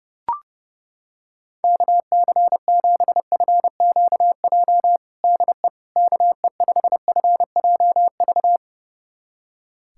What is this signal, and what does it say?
Morse code, heard as a keyed tone, about 30 wpm, 700 Hz. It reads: KC7FQJ DE KE5FJV
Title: USA Callsign DE Callsign - 6 Characters - 30wpm